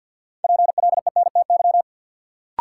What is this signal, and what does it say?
Morse code, heard as a keyed tone, about 53 wpm, 700 Hz. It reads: PLENTY
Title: Sentences from Top 1300 Words - New Word Drill - 50wpm